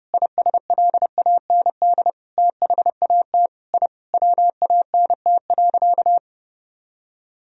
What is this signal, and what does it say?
Morse code, heard as a keyed tone, about 30 wpm, 700 Hz. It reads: ISLAND THAT I WANT.
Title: Sentences from Top 1000 Words - Review Part 3 - 30wpm